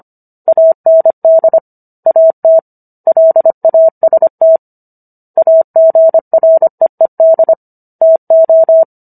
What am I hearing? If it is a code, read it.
Morse code, heard as a keyed tone, about 25 wpm, 650 Hz. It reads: AND AT LAST AGREED TO